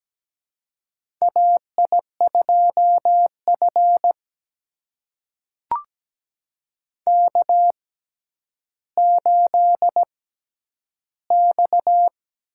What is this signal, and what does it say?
Morse code, heard as a keyed tone, about 17 wpm, spaced as for 10 wpm, 700 Hz. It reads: AI2F K8X
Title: USA Call Signs - 4 Characters - 6x Character Spacing - 17wpm